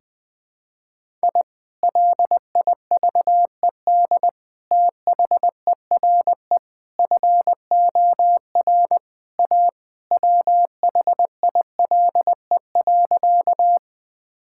Morse code, heard as a keyed tone, about 21 wpm, 700 Hz. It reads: I LIVED THERE FOR A WHILE.
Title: Sentences with 3-Word Phrases - Spoken Beginning - 20wpm